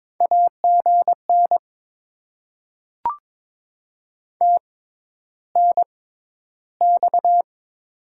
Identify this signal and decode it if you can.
Morse code, heard as a keyed tone, about 22 wpm, spaced as for 11 wpm, 700 Hz. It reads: AGN TNX
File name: Top 20 QSO Elements - 6x Character Spacing - 22wpm